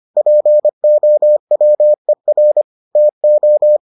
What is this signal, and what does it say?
Morse code, heard as a keyed tone, about 25 wpm, 600 Hz. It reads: POWER TO